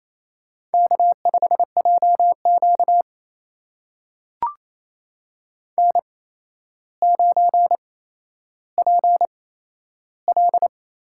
Morse code, heard as a keyed tone, about 28 wpm, spaced as for 12 wpm, 700 Hz. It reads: K5JQ N9PL
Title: USA Call Signs - 4 Characters - 8x Character Spacing - 28wpm